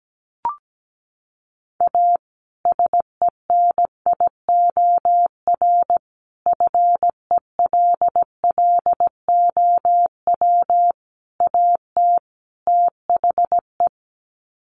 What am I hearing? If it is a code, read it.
Morse code, heard as a keyed tone, about 17 wpm, 700 Hz. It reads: A SENIOR FELLOW AT THE
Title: Sets of 5 Words Encore 17wpm